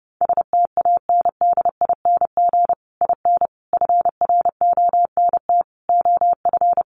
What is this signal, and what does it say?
Morse code, heard as a keyed tone, about 30 wpm, 700 Hz. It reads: STANDING IN FRONT OF
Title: Sets of 5 Words Encore2 30wpm